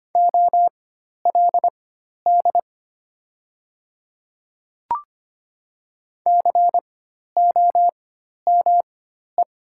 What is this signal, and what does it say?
Morse code, heard as a keyed tone, about 25 wpm, spaced as for 12 wpm, 700 Hz. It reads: OLD COME
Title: 4x Character Spacing - Top 200 Words - 25wpm